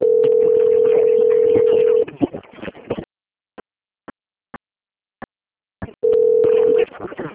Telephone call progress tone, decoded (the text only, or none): ringback tone